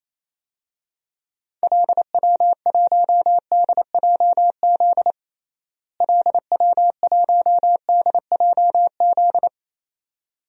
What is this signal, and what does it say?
Morse code, heard as a keyed tone, about 28 wpm, 700 Hz. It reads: LW1DJZ LW1DJZ
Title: DXCC-100 Call Signs - 6 Characters - Repeated 2x - 28wpm